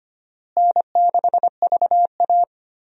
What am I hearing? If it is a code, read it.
Morse code, heard as a keyed tone, about 25 wpm, 700 Hz. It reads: N6VA
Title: USA Call Signs - 4 Characters - Repeated 3x - 25wpm